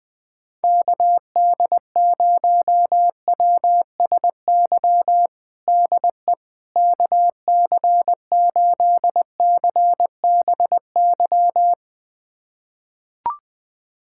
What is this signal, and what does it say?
Morse code, heard as a keyed tone, about 21 wpm, 700 Hz. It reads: KD0WSY DE KC8CBY
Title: USA Callsign DE Callsign 20wpm